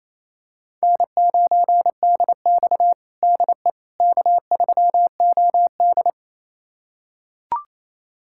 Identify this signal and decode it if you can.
Morse code, heard as a keyed tone, about 28 wpm, 700 Hz. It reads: N9DX DE K3OD
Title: USA Callsign DE Callsign - 4 Characters - 28wpm